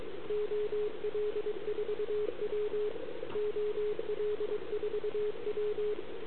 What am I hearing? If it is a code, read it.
Morse code, heard as a keyed tone, about 24 wpm, 420 Hz, with radio noise. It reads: OL4W OL4W
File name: CQWWCW2021 / JA3YBK QSO